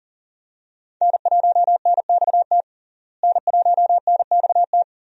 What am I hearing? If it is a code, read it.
Morse code, heard as a keyed tone, about 40 wpm, 700 Hz. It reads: N1NXT N1NXT
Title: USA Call Signs - 5 Characters - Repeated 2x - 40wpm